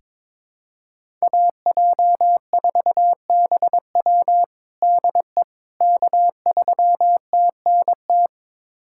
Morse code, heard as a keyed tone, about 22 wpm, 700 Hz. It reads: AJ4BW DE K3TNT
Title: USA Callsign DE Callsign - 5 Characters - 22wpm